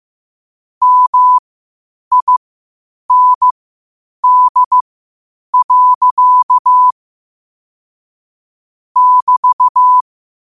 Morse code, heard as a keyed tone, about 15 wpm, spaced as for 8 wpm, 1000 Hz. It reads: MIND. <BT>
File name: qotdfiles / 8wpm